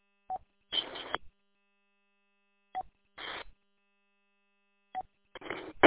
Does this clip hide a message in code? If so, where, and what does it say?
No.